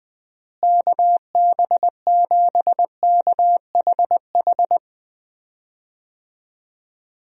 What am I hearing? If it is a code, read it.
Morse code, heard as a keyed tone, about 20 wpm, 700 Hz. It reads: KB7KHH
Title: USA Call Signs - 6 Characters - Repeated 2x - 20wpm